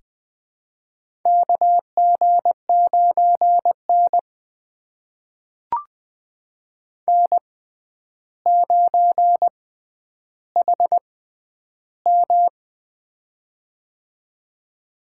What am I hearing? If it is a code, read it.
Morse code, heard as a keyed tone, about 20 wpm, spaced as for 10 wpm, 700 Hz. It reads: KG9N N9HM
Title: USA Call Signs - 6x Character Spacing - 20wpm